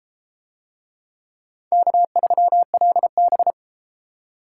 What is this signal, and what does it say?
Morse code, heard as a keyed tone, about 29 wpm, 700 Hz. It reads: K3LB